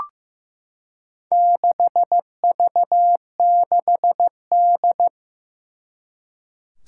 Morse code, heard as a keyed tone, about 16 wpm, 700 Hz. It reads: 6V6D
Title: Four Challenging Letter-Number Combinations 15wpm